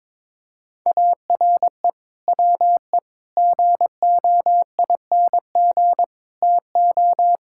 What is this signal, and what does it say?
Morse code, heard as a keyed tone, about 22 wpm, 700 Hz. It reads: ARE WE GOING TO
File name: practice / 22wpm / Sentences with 3-Word Phrases - 22wpm